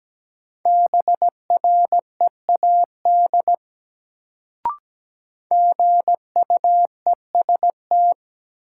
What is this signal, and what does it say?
Morse code, heard as a keyed tone, about 17 wpm, 700 Hz. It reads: BREAD GUEST
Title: Sentences from Top 1600 Words - New Word Drill Alt - 17wpm